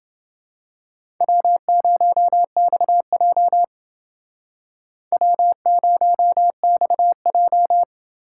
Morse code, heard as a keyed tone, about 29 wpm, 700 Hz. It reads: W0XJ W0XJ